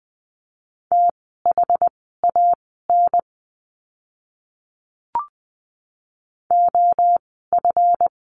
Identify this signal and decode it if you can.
Morse code, heard as a keyed tone, about 20 wpm, spaced as for 15 wpm, 700 Hz. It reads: THAN OF
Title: 185 Lesson 30f - Letter b - Words 20wpm